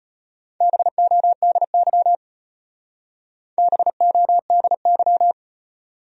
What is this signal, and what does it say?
Morse code, heard as a keyed tone, about 32 wpm, 700 Hz. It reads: BODY BODY